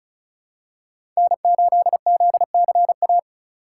Morse code, heard as a keyed tone, about 34 wpm, 700 Hz. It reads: N8ZCA
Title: USA Call Signs - 5 Characters - 4x Character Spacing - 35wpm